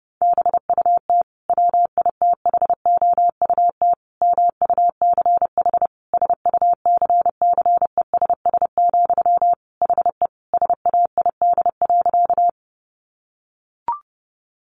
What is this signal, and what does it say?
Morse code, heard as a keyed tone, about 30 wpm, 700 Hz. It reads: BUT WITHOUT MUCH SUCCESS, HE SAID.
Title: Sentences from Top 900 Words 30wpm